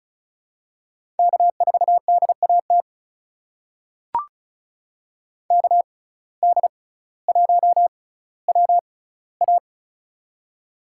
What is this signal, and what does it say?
Morse code, heard as a keyed tone, about 36 wpm, spaced as for 18 wpm, 700 Hz. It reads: K4DAT KD1WA
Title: USA Call Signs - 5 Characters - 6x Character Spacing - 35wpm